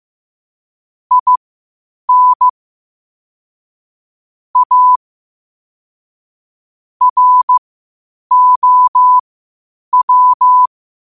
Morse code, heard as a keyed tone, about 15 wpm, spaced as for 8 wpm, 1000 Hz. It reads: IN A ROW
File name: qotdfiles / 8wpm